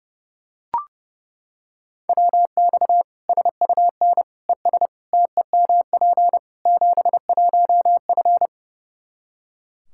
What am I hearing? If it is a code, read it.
Morse code, heard as a keyed tone, about 30 wpm, 700 Hz. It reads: WX SUN ES TEMP 71F
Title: Practice QSO - Exchange 2 - Info - 30wpm